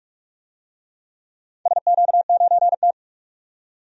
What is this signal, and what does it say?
Morse code, heard as a keyed tone, about 41 wpm, 700 Hz. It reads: SQ9T
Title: DXCC-100 Call Signs - 4 Characters - 45wpm